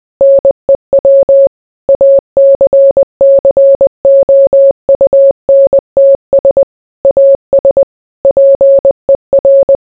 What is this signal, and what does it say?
Morse code, heard as a keyed tone, about 20 wpm, 550 Hz. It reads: NEW ACCOUNTS AS PER